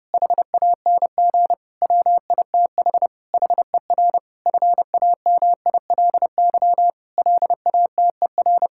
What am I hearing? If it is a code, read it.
Morse code, heard as a keyed tone, about 32 wpm, 700 Hz. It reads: HANG WITH HER FAMILY LATER
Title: Sentences from Top 1600 Words 30wpm